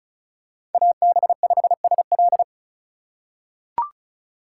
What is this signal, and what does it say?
Morse code, heard as a keyed tone, about 34 wpm, 700 Hz. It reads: AB5SL
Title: USA Call Signs - 5 Characters - 4x Character Spacing - 35wpm